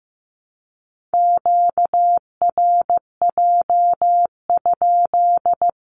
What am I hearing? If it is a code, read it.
Morse code, heard as a keyed tone, about 15 wpm, 700 Hz. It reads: QRJ?
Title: Q-Codes Repeated 3x 15wpm